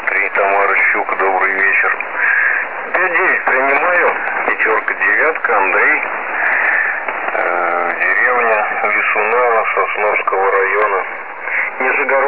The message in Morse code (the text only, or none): none